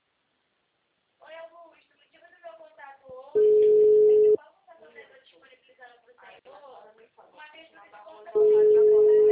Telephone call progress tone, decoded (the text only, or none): ringback tone